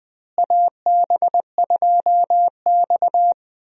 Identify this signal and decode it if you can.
Morse code, heard as a keyed tone, about 20 wpm, 700 Hz. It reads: AB2X
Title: USA Call Signs - 4 Characters - Repeated 2x - 20wpm